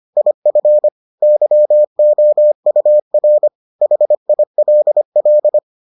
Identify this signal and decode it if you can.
Morse code, heard as a keyed tone, about 25 wpm, 600 Hz. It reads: IF YOUR HILL